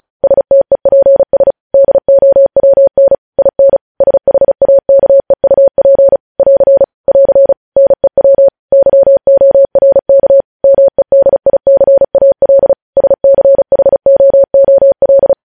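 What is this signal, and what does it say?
Morse code, heard as a keyed tone, about 36 wpm, 550 Hz. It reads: STEPS DOWN IN SHAKEUP <AR> <AR> NEW YORK MEDICAL SCHOOL
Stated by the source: Morse code practice reading news headlines at 35 words per minute.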